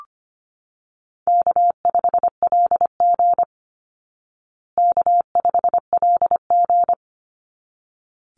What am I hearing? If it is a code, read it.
Morse code, heard as a keyed tone, about 25 wpm, 700 Hz. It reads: K5LG K5LG
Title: Easy Call Signs Repeated 2x - 25wpm